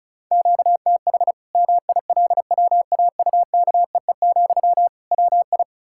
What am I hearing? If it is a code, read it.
Morse code, heard as a keyed tone, about 36 wpm, 700 Hz. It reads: QTH MILWAUKEE, WI
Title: Practice QSO - Exchange 1 - Beginning - 35wpm